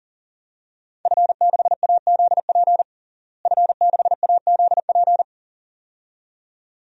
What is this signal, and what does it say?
Morse code, heard as a keyed tone, about 40 wpm, 700 Hz. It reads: F6AZP F6AZP
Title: DXCC-100 Call Signs - 5 Characters - Repeated 2x - 40wpm